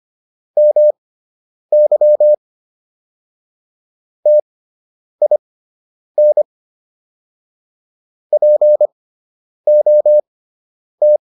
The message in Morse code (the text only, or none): MY TIN POT